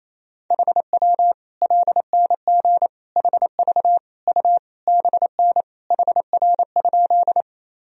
Morse code, heard as a keyed tone, about 28 wpm, 700 Hz. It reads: HW LNG HV U BN HR?
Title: Sentences from Top 200 Words and Abbreviations 28wpm